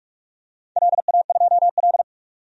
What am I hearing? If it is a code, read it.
Morse code, heard as a keyed tone, about 41 wpm, 700 Hz. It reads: LA2L
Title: DXCC-100 Call Signs - 4 Characters - 45wpm